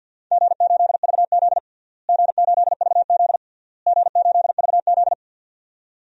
Morse code, heard as a keyed tone, about 53 wpm, 700 Hz. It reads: G8VZ G8VZ G8VZ
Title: DXCC-100 Call Signs - 4 Characters - Repeated 3x - 50wpm